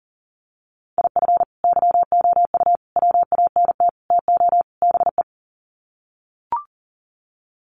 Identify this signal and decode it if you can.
Morse code, heard as a keyed tone, about 40 wpm, 700 Hz. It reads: IF YOU WANT TO BE
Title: Sets of 5 Words Encore 40wpm